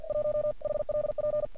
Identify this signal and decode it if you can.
Morse code, heard as a keyed tone, about 46 wpm, 600 Hz. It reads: 1 HLP
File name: archived / CRSnet